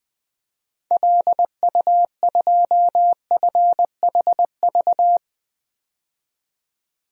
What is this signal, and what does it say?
Morse code, heard as a keyed tone, about 20 wpm, 700 Hz. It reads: LU2FHV
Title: DXCC-100 Call Signs - 6 Characters - 20wpm